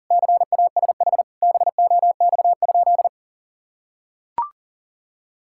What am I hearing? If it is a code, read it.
Morse code, heard as a keyed tone, about 37 wpm, 700 Hz. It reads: CASH BOX?
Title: Sentences from Top 1000 Words - Review Part 5 - 40wpm